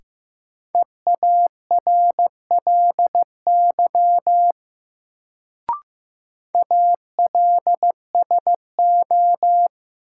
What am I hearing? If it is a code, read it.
Morse code, heard as a keyed tone, about 15 wpm, 700 Hz. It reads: EARLY ALSO
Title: Sentences from Top 400 Words - New Word Drill - 15wpm